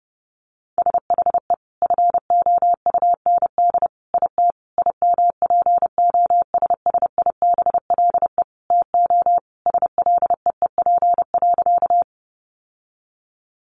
Morse code, heard as a keyed tone, about 30 wpm, 700 Hz. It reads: SHE FOUND IT IMPOSSIBLE TO SLEEP.
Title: Sentences from Top 1100 Words 30wpm